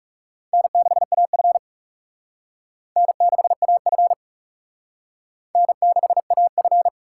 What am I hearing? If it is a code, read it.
Morse code, heard as a keyed tone, about 36 wpm, 700 Hz. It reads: N6AF N6AF N6AF
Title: Speed Racing - USA 1x2 Call Signs 30wpm